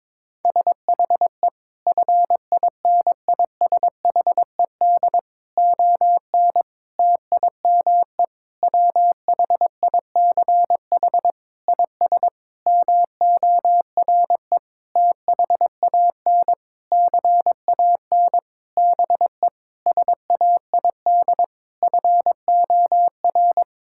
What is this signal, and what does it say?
Morse code, heard as a keyed tone, about 22 wpm, 700 Hz. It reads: SHE FINISHED ON TIME WHICH IS MORE THAN CAN BE SAID FOR
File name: practice / 22wpm / Sentences with 8-Word Phrases - 22wpm